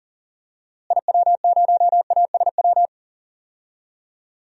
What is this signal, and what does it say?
Morse code, heard as a keyed tone, about 40 wpm, 700 Hz. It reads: IW0ASW
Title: DXCC-100 Call Signs - 6 Characters - Repeated 2x - 40wpm